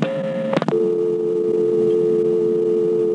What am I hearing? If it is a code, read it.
Dial tone (North America), 350 + 440 Hz.